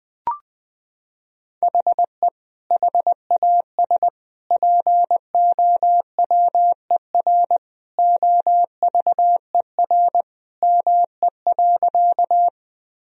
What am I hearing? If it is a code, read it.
Morse code, heard as a keyed tone, about 19 wpm, 700 Hz. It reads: HE HAS POWER OVER ME.
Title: Sentences from Top 1000 Words - Review Part 2 - 20wpm